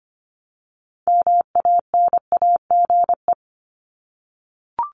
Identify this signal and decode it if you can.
Morse code, heard as a keyed tone, about 24 wpm, 700 Hz. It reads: MANAGE
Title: Top 1251-1500 Words 25wpm